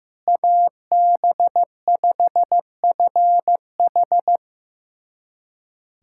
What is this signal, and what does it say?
Morse code, heard as a keyed tone, about 15 wpm, 700 Hz. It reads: AB5FH
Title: USA Callsign DE Callsign 15wpm